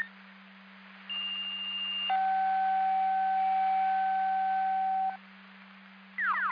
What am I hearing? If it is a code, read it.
Two-tone page, heard as a one-second tone, then a three-second tone, about 2800 then 760 Hz. It is heard from 1.1 to 5.2 s.
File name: Hamburg FD Siren 2026 06 20 12 01 22